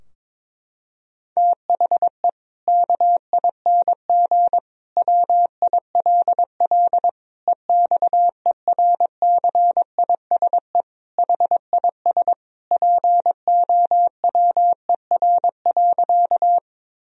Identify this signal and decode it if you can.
Morse code, heard as a keyed tone, about 21 wpm, 700 Hz. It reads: THE KING WILL EXERCISE HIS POWER.
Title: Sentences from Top 1000 Words - Review Part 4 - 22wpm